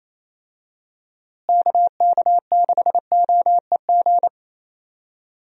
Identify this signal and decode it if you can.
Morse code, heard as a keyed tone, about 28 wpm, 700 Hz. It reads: KK6OEG
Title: USA Call Signs - 6 Characters - Repeated 3x - 28wpm